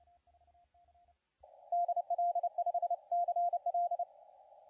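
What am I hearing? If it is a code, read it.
Morse code, heard as a keyed tone, about 30 wpm, 700 Hz, with radio noise. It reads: DL5CL
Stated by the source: CW contest audio recorded at LA4C during the SAC CW contest.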